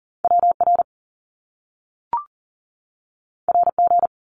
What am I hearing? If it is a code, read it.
Morse code, heard as a keyed tone, about 41 wpm, 700 Hz. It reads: WR RG
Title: Top Two Letter Combinations in Words 40wpm